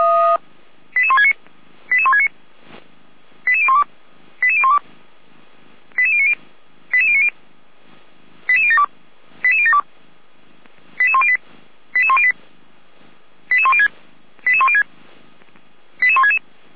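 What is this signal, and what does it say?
Five-tone selective call (ZVEI), heard as quick runs of five tones, tones of about 70 ms. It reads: 88169 88169 88012 88012 88090 88090 88072 88072 88108 88108 88106 88106 88166
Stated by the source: ZVEI five-tone selective call received on an SDR.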